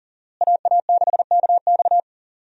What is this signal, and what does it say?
Morse code, heard as a keyed tone, about 40 wpm, 700 Hz. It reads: AA6KX
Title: USA Call Signs - 5 Characters - Repeated 2x - 40wpm